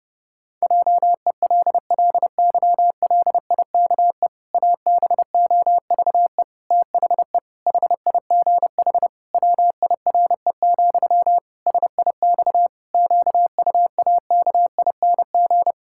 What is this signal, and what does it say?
Morse code, heard as a keyed tone, about 32 wpm, 700 Hz. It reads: JELLYLIKE ABOVE THE HIGH WIRE, SIX QUAKING
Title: Pangram Challenge v2 - 30wpm